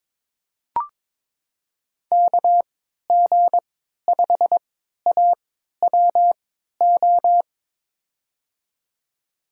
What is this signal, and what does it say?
Morse code, heard as a keyed tone, about 23 wpm, spaced as for 13 wpm, 700 Hz. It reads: KG5AWO
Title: USA Call Signs - 6 Characters - 3x Character Spacing - 22wpm